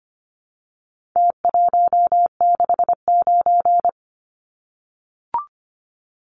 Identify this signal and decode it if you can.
Morse code, heard as a keyed tone, about 25 wpm, 700 Hz. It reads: T169
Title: Numbers - Contest Style - 4-Digit 25wpm